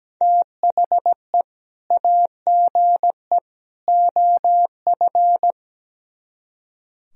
Morse code, heard as a keyed tone, about 20 wpm, 700 Hz. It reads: THE AGE OF
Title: Speed-Racing - Sets of 3 Words 17wpm